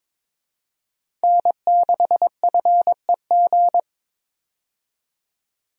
Morse code, heard as a keyed tone, about 22 wpm, 700 Hz. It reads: N6FEG
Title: USA Call Signs - 5 Characters - 22wpm